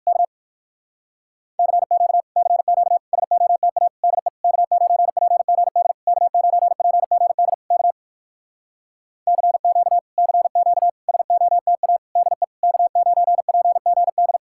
Morse code, heard as a keyed tone, about 65 wpm, spaced as for 57 wpm, 700 Hz. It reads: K CQ CQ SOTA DE K9PGD K9PGD K CQ CQ SOTA DE K9PGD